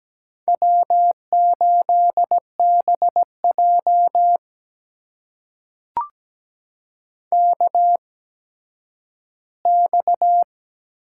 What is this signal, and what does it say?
Morse code, heard as a keyed tone, about 17 wpm, spaced as for 10 wpm, 700 Hz. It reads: W8BJ KX
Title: USA Call Signs - 4 Characters - 8x Character Spacing - 17wpm